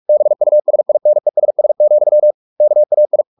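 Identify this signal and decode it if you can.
Morse code, heard as a keyed tone, about 44 wpm, 600 Hz. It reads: BUSINESS, KAI
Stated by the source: Morse practice at 45 wpm, the novel Never by Ken Follett in English, from CW med Gnister.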